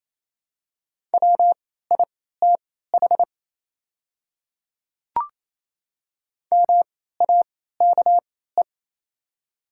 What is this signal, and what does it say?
Morse code, heard as a keyed tone, about 28 wpm, spaced as for 16 wpm, 700 Hz. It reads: WITH MAKE
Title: 3x Character Spacing - Top 100 Words - 28wpm